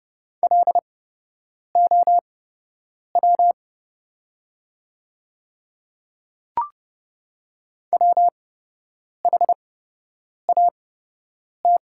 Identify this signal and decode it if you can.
Morse code, heard as a keyed tone, about 30 wpm, spaced as for 8 wpm, 690 Hz. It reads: LOW WHAT
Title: 8x Character Spacing - Top 200 Words - 30wpm